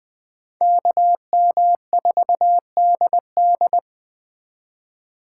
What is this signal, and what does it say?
Morse code, heard as a keyed tone, about 24 wpm, 700 Hz. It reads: KM4DD
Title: Speed Racing - USA Call Signs 20wpm